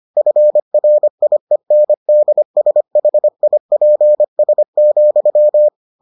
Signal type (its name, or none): Morse code